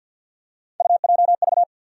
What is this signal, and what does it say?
Morse code, heard as a keyed tone, about 53 wpm, 700 Hz. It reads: UJV
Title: USA Call Sign Postfixes - 3 Letters 50wpm